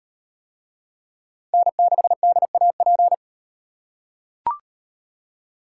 Morse code, heard as a keyed tone, about 36 wpm, 700 Hz. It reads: N6DAP